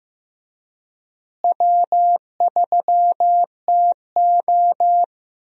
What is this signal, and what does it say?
Morse code, heard as a keyed tone, about 15 wpm, 700 Hz. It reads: W3TO
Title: USA Call Signs - 4 Characters - Repeated 3x - 15wpm